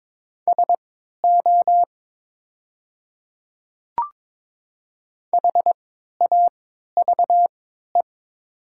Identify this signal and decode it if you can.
Morse code, heard as a keyed tone, about 22 wpm, spaced as for 12 wpm, 700 Hz. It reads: SO HAVE
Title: 3x Character Spacing - Top 100 Words - 22wpm